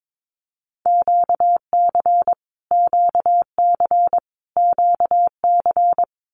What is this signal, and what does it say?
Morse code, heard as a keyed tone, about 22 wpm, 700 Hz. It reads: QC QC QC
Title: Canadian Provinces and Territories - Repeated 3x - 22wpm